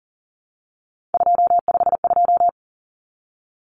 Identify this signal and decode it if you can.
Morse code, heard as a keyed tone, about 40 wpm, 700 Hz. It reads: 252
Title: Common Cut Numbers - 3-Digit 40wpm